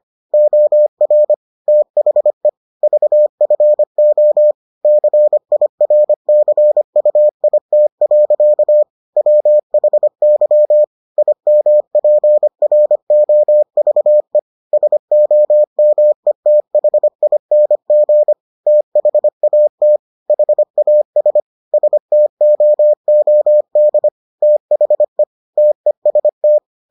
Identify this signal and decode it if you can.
Morse code, heard as a keyed tone, about 25 wpm, 600 Hz. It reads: OR THE VFO CIRCUIT. WHY IMPROVE SOMETHING THAT HAS STOOD THE TEST